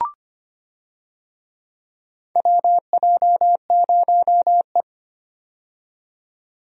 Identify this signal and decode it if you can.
Morse code, heard as a keyed tone, about 25 wpm, 700 Hz. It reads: WJ0E
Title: USA Call Signs - 4 Characters - 25wpm